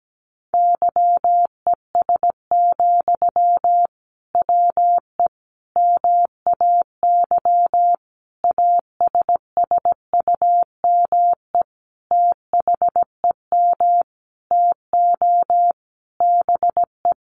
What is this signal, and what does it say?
Morse code, heard as a keyed tone, about 17 wpm, 700 Hz. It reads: YES, WE MAY ASSUME THEM TO BE
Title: Sentences from Top 1500 Words 17wpm